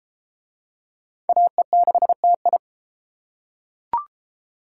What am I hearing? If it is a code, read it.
Morse code, heard as a keyed tone, about 31 wpm, 700 Hz. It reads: AE6TI